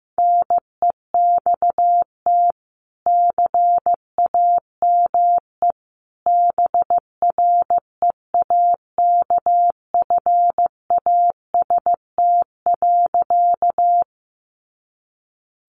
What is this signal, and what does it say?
Morse code, heard as a keyed tone, about 15 wpm, 700 Hz. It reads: NEXT CAME BREAKFAST.
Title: Sentences from Top 1500 Words 15wpm